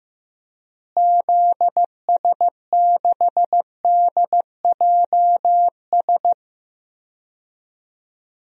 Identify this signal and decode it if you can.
Morse code, heard as a keyed tone, about 16 wpm, 700 Hz. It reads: ZS6DJS
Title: DXCC-100 Call Signs - 6 Characters - Repeated 2x - 15wpm